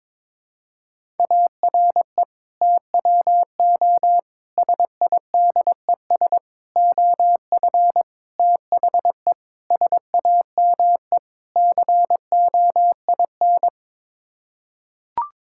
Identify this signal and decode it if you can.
Morse code, heard as a keyed tone, about 22 wpm, 700 Hz. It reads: ARE TWO SIDES OF THE SAME COIN
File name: practice / 22wpm / Sets of 7 Words - 22wpm